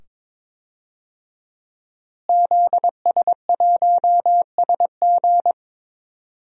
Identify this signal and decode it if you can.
Morse code, heard as a keyed tone, about 22 wpm, 700 Hz. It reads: ZS1SG
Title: DXCC-100 Call Signs - 5 Characters - Repeated 2x - 22wpm